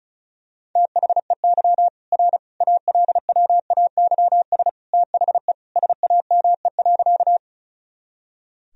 Morse code, heard as a keyed tone, about 36 wpm, 700 Hz. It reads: THEY R ALWAYS THE SAME.
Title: Sentences from Top 300 Words and Abbreviations 35wpm